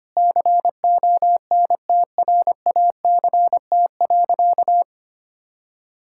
Morse code, heard as a keyed tone, about 25 wpm, 700 Hz. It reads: CONTRACT.
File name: practice / 25wpm / Sentences with 9-Word Phrases - 25wpm